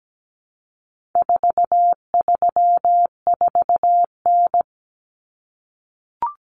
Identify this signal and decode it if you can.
Morse code, heard as a keyed tone, about 17 wpm, 700 Hz. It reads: 434N
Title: Common Cut Numbers - 4-Digit 17wpm